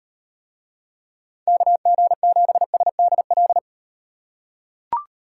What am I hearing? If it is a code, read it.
Morse code, heard as a keyed tone, about 34 wpm, 700 Hz. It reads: KG7SDL